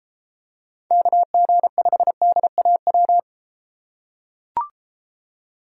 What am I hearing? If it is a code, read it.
Morse code, heard as a keyed tone, about 32 wpm, 700 Hz. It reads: KG5DAW